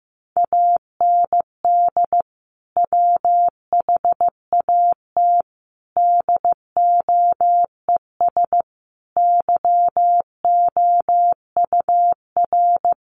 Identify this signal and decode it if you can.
Morse code, heard as a keyed tone, about 15 wpm, 700 Hz. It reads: AND WHAT DOES YOUR
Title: Sentences from Top 1200 Words 15wpm